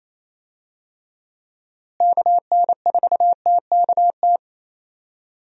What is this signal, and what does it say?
Morse code, heard as a keyed tone, about 28 wpm, 700 Hz. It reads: KN4TKT
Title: USA Call Signs - 6 Characters - Repeated 3x - 28wpm